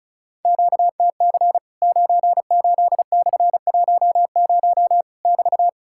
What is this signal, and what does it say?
Morse code, heard as a keyed tone, about 34 wpm, 700 Hz. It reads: QTC 98/10 <BT>
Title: Worked All Europe DX Contest - QTC Exchange - 35wpm